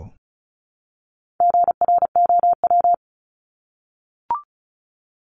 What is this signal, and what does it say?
Morse code, heard as a keyed tone, about 34 wpm, 700 Hz. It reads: GROW
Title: Top 300 Words Repeated 35wpm